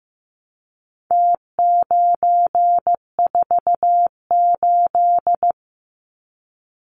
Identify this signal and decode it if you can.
Morse code, heard as a keyed tone, about 15 wpm, 700 Hz. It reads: T948
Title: Numbers - Contest Style - 4-Digit 15wpm